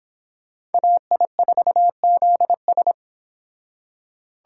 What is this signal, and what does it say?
Morse code, heard as a keyed tone, about 29 wpm, 700 Hz. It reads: AI4ZS